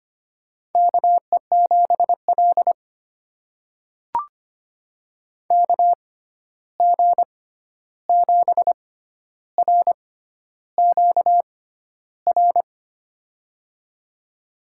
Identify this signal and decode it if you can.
Morse code, heard as a keyed tone, about 25 wpm, spaced as for 11 wpm, 700 Hz. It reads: KE7L KG7RQR
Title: USA Call Signs - 6x Character Spacing - 25wpm